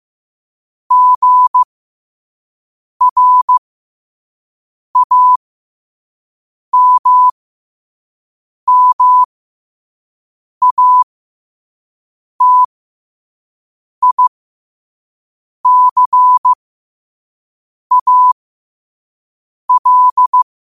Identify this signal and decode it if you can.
Morse code, heard as a keyed tone, about 15 wpm, spaced as for 5 wpm, 1000 Hz. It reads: GRAMMATICAL
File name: qotdfiles / 5wpm